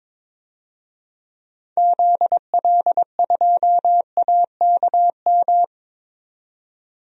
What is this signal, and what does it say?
Morse code, heard as a keyed tone, about 21 wpm, 700 Hz. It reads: ZL2AKM
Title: DXCC-100 Call Signs - 6 Characters - 22wpm